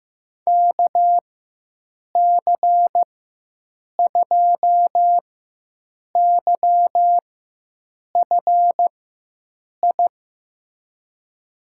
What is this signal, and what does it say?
Morse code, heard as a keyed tone, about 15 wpm, spaced as for 7 wpm, 700 Hz. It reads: KC2YFI
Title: USA Call Signs - 6 Characters - 4x Character Spacing - 15wpm